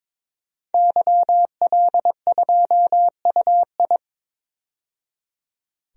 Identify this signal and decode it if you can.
Morse code, heard as a keyed tone, about 22 wpm, 700 Hz. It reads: YL2UI
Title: DXCC-100 Call Signs - 5 Characters - 22wpm